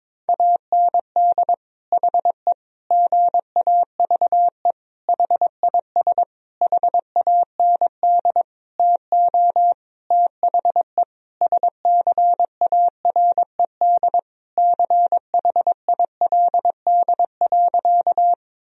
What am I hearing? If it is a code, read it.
Morse code, heard as a keyed tone, about 22 wpm, 700 Hz. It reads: AND HE GAVE HIS HAND TO THE SCARED CHILD.
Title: Sentences from Top 1000 Words - Review Part 4 - 22wpm